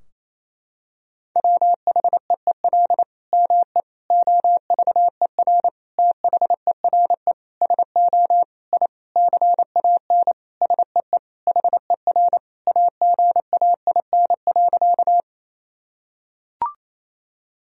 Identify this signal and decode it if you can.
Morse code, heard as a keyed tone, about 28 wpm, 700 Hz. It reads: WHEEL ME OVER THERE SO I CAN SEE HER AGAIN.
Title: Sentences from Top 1000 Words - Review Part 3 - 28wpm